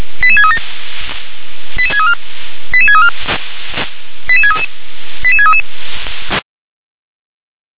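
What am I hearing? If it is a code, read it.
Five-tone selective call (ZVEI), heard as quick runs of five tones, tones of about 80 ms. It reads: 88637 88634 88634 88633 88633
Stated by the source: ZVEI five-tone selective call received on SDR.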